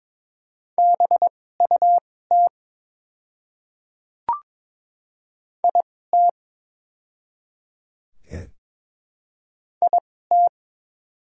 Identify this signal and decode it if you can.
Morse code, heard as a keyed tone, about 22 wpm, spaced as for 16 wpm, 690 Hz. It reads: BUT IT IT
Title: 2x Character Spacing - Top 100 Words - 22wpm